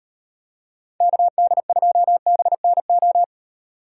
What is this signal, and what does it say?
Morse code, heard as a keyed tone, about 32 wpm, 700 Hz. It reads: KD2BNO